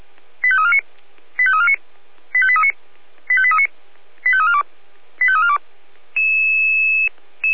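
Five-tone selective call (ZVEI), heard as quick runs of five tones, tones of about 70 ms. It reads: 85349 85349 85839 85839 85342 85342
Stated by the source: ZVEI five-tone call received by SDR.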